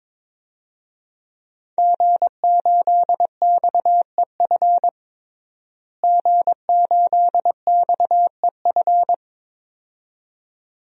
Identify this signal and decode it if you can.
Morse code, heard as a keyed tone, about 22 wpm, 700 Hz. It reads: G8XEF G8XEF
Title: DXCC-100 Call Signs - 5 Characters - Repeated 2x - 22wpm